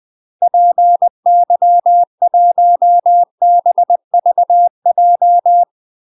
Morse code, heard as a keyed tone, about 20 wpm, 700 Hz. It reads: PY1BVJ
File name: UK MASTER QSOs TRAINING / 20WPM / UK MASTER QSOs 0036